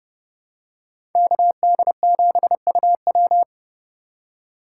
Morse code, heard as a keyed tone, about 32 wpm, 700 Hz. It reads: KD7UW
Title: USA Call Signs - 8x Character Spacing - 30wpm